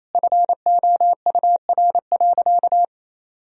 Morse code, heard as a keyed tone, about 28 wpm, 700 Hz. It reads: FOUR.
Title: Sentences from Top 1000 Words - Review Part 2 - 28wpm